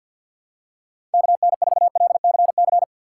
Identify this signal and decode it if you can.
Morse code, heard as a keyed tone, about 50 wpm, 700 Hz. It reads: KN4LCC
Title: USA Call Signs - 6x Character Spacing - 50wpm